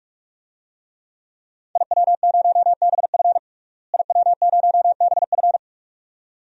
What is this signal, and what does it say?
Morse code, heard as a keyed tone, about 45 wpm, 700 Hz. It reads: IW0BF IW0BF
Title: DXCC-100 Call Signs - 5 Characters - Repeated 2x - 45wpm